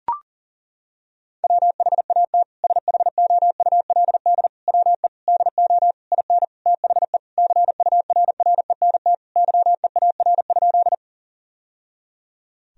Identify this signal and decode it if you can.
Morse code, heard as a keyed tone, about 40 wpm, 700 Hz. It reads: WHAT SHOULD WE DO IN THE CURRENT YEAR?
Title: Sentences from Top 1000 Words - Review Part 4 - 40wpm